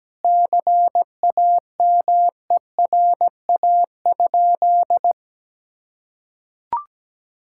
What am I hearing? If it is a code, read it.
Morse code, heard as a keyed tone, about 17 wpm, 700 Hz. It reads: CAMERA?
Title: Sentences from Top 1000 Words - Review Part 5 - 17wpm